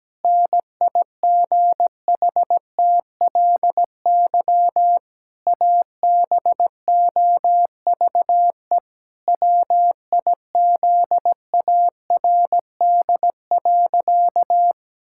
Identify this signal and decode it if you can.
Morse code, heard as a keyed tone, about 17 wpm, 700 Hz. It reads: NIGHTLY ABOVE WIZARD.
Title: Pangram Challenge v2 - 17wpm